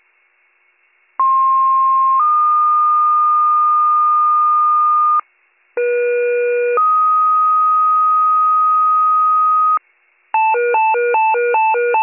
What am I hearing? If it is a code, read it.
Two-tone page, heard as a one-second tone, then a three-second tone, about 1060 then 1220 Hz, about 480 then 1250 Hz. It is heard from 1.2 to 9.8 s.